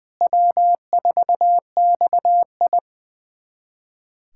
Morse code, heard as a keyed tone, about 21 wpm, 700 Hz. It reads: W4XI
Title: USA Callsign DE Callsign - 4 Characters - 20wpm